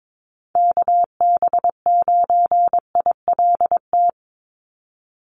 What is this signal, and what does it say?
Morse code, heard as a keyed tone, about 22 wpm, 700 Hz. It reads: KB9ILT
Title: NA SOTA Activator Callsigns 22wpm